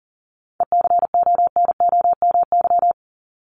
Morse code, heard as a keyed tone, about 40 wpm, 700 Hz. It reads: ECONOMY
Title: Top 501-750 Words 40wpm